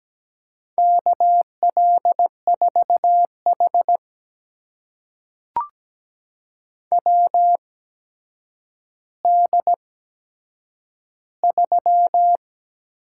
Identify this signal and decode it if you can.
Morse code, heard as a keyed tone, about 17 wpm, spaced as for 8 wpm, 700 Hz. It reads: KL4H WD3
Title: USA Call Signs - 4 Characters - 8x Character Spacing - 17wpm